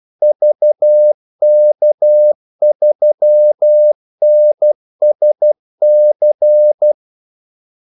Morse code, heard as a keyed tone, about 12 wpm, 600 Hz. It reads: VK3NSC
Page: 2 ekstra ordmellemrum mellem kaldesignalerne